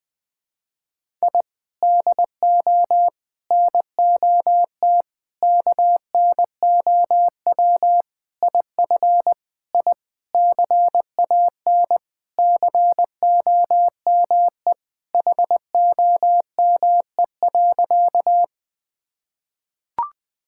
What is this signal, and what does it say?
Morse code, heard as a keyed tone, about 20 wpm, 700 Hz. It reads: I DO NOT KNOW IF I CAN COME HOME.
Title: Sentences with 5-Word Phrases - 20wpm